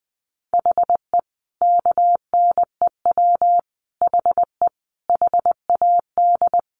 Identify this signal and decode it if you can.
Morse code, heard as a keyed tone, about 20 wpm, 700 Hz. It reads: HE KNEW HE HAD
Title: Sentences from Top 600 Words 20wpm